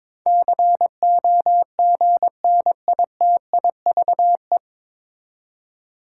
Morse code, heard as a keyed tone, about 22 wpm, 700 Hz. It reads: COGNITIVE
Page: Morse Code Ninja practice 22wpm